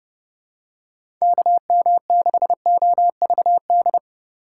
Morse code, heard as a keyed tone, about 30 wpm, 700 Hz. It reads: KM6OVD
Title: USA Call Signs - 6 Characters - 2x Character Spacing - 30wpm